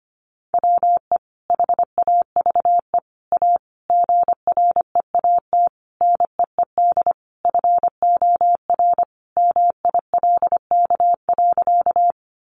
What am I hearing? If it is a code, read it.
Morse code, heard as a keyed tone, about 25 wpm, 700 Hz. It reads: WE HAVE A GREAT NEED FOR MILK.
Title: Sentences from Top 1500 Words 25wpm